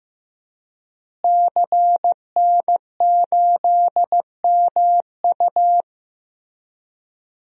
Morse code, heard as a keyed tone, about 15 wpm, 700 Hz. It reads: CN8MU
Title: DXCC-100 Call Signs - 5 Characters - Repeated 3x - 15wpm